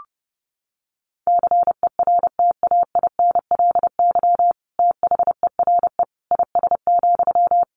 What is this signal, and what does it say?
Morse code, heard as a keyed tone, about 30 wpm, 700 Hz. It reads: CERTAINLY THERE IS,
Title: Sentences from Top 1100 Words 30wpm